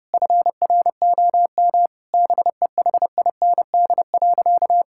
Morse code, heard as a keyed tone, about 30 wpm, 700 Hz. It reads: FROM BEHIND.
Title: Sentences from Top 1600 Words 30wpm